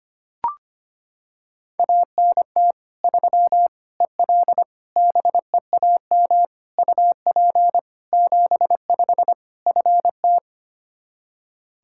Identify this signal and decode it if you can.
Morse code, heard as a keyed tone, about 25 wpm, 700 Hz. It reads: ANT 3 EL BEAM UP 75 FT
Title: Practice QSO - Exchange 2 - Info - 25wpm